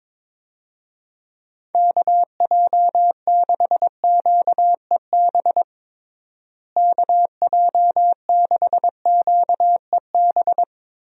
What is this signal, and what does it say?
Morse code, heard as a keyed tone, about 22 wpm, 700 Hz. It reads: KJ6QEB KJ6QEB
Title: USA Call Signs - 6 Characters - Repeated 2x - 22wpm